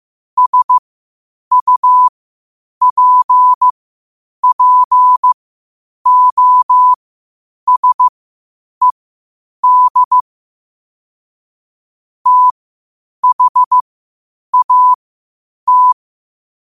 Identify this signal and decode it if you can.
Morse code, heard as a keyed tone, about 15 wpm, spaced as for 8 wpm, 1000 Hz. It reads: SUPPOSED THAT
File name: qotdfiles / 8wpm